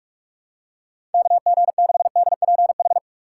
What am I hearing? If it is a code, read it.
Morse code, heard as a keyed tone, about 45 wpm, 700 Hz. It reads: KG6DPH
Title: USA Call Signs - 6 Characters - 4x Character Spacing - 45wpm